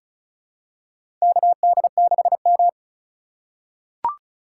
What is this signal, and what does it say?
Morse code, heard as a keyed tone, about 36 wpm, 690 Hz. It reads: KD6M
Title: USA Call Signs - 4 Characters - 2x Character Spacing - 35wpm